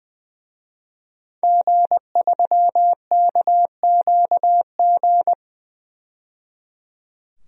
Morse code, heard as a keyed tone, about 20 wpm, 700 Hz. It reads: G3KQG
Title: DXCC-100 Call Signs - 5 Characters - 20wpm